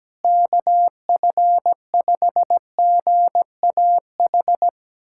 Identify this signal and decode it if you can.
Morse code, heard as a keyed tone, about 17 wpm, 700 Hz. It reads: KF5GAH
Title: USA Callsign DE Callsign 17wpm